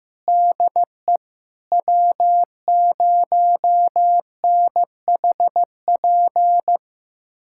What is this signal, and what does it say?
Morse code, heard as a keyed tone, about 16 wpm, 700 Hz. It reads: DE W0NHP
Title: USA Callsign DE Callsign - 5 Characters - 15wpm